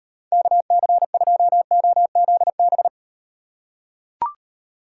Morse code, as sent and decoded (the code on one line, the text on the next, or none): -.- -.-. ..--- --- --.. -...
KC2OZB